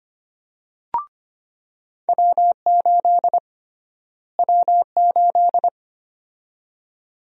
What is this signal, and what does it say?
Morse code, heard as a keyed tone, about 25 wpm, 700 Hz. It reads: W8 W8
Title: USA Call Sign Prefixes - Repeated 2x - 25wpm